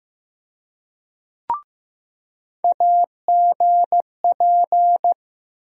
Morse code, heard as a keyed tone, about 15 wpm, 700 Hz. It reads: AGP
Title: USA Call Sign Postfixes - 3 Letters 15wpm